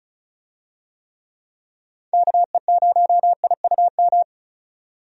Morse code, heard as a keyed tone, about 34 wpm, 700 Hz. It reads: KE0IUM
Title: USA Call Signs - 6 Characters - 35wpm